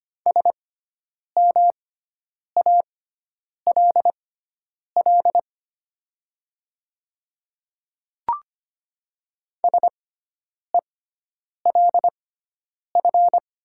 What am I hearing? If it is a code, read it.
Morse code, heard as a keyed tone, about 25 wpm, spaced as for 9 wpm, 700 Hz. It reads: SMALL SELF
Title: 6x Character Spacing - Top 500 Words - 25wpm